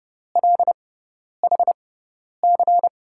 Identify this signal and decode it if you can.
Morse code, heard as a keyed tone, about 30 wpm, spaced as for 10 wpm, 700 Hz. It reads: LHC
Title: USA Call Signs - 6x Character Spacing - 30wpm